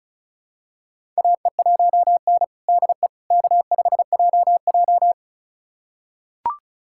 Morse code, heard as a keyed tone, about 36 wpm, 700 Hz. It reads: AE1N DE K5JJ
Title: USA Callsign DE Callsign - 4 Characters - 35wpm